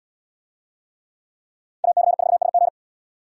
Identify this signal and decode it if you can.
Morse code, heard as a keyed tone, about 72 wpm, 700 Hz. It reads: AC4SG